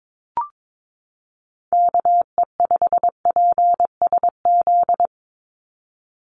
Morse code, heard as a keyed tone, about 22 wpm, 700 Hz. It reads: KE5PSZ
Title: USA Call Signs 22wpm